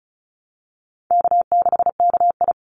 Morse code, heard as a keyed tone, about 36 wpm, 700 Hz. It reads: K6KI
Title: Call Signs - 35wpm